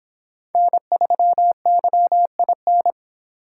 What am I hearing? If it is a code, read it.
Morse code, heard as a keyed tone, about 25 wpm, 700 Hz. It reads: N3YIN